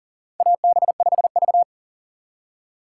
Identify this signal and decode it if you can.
Morse code, heard as a keyed tone, about 40 wpm, 700 Hz. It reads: AB5V
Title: USA Call Signs - 4 Characters - 6x Character Spacing - 40wpm